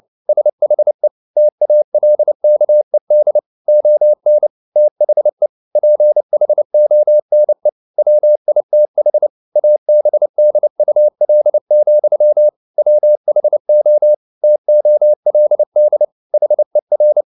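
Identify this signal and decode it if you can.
Morse code, heard as a keyed tone, about 29 wpm, 600 Hz. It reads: SHE TALKED ON THE PHONE WITH ABDUL, WHO TOLD HER
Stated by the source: Morse code at 29 wpm, sending Ken Follett's novel Never in English, from CW med Gnister.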